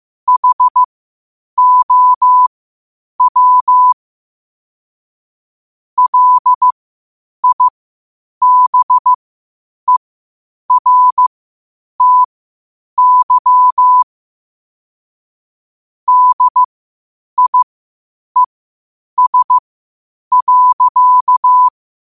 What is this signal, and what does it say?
Morse code, heard as a keyed tone, about 15 wpm, spaced as for 8 wpm, 1000 Hz. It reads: HOW LIBERTY DIES.
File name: qotdfiles / 8wpm